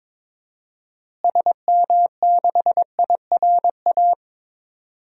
Morse code, heard as a keyed tone, about 22 wpm, 700 Hz. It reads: SM6IRA
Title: DXCC-100 Call Signs - 6 Characters - Repeated 3x - 22wpm